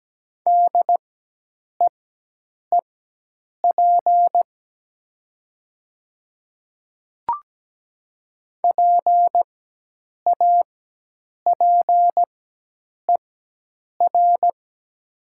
Morse code, heard as a keyed tone, about 17 wpm, spaced as for 8 wpm, 700 Hz. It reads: DEEP PAPER
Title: 4x Character Spacing - Top 500 Words - 17wpm